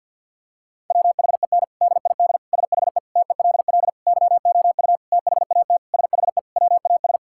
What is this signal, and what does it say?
Morse code, heard as a keyed tone, about 50 wpm, 700 Hz. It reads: WHEN DID SHE TELL YOU THAT SHE WAS
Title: Sentences from Top 1000 Words - Review Part 6 - 50wpm